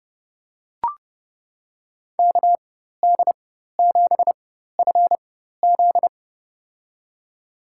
Morse code, heard as a keyed tone, about 30 wpm, spaced as for 14 wpm, 700 Hz. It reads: KD7FZ
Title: USA Call Signs - 5 Characters - 4x Character Spacing - 30wpm